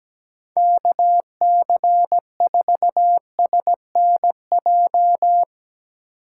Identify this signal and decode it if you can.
Morse code, heard as a keyed tone, about 17 wpm, 700 Hz. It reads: KC4SNJ
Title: USA Call Signs - 6 Characters - 8x Character Spacing - 17wpm